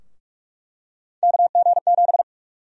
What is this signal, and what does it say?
Morse code, heard as a keyed tone, about 48 wpm, 700 Hz. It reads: KG7
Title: USA Call Sign Prefixes - 45wpm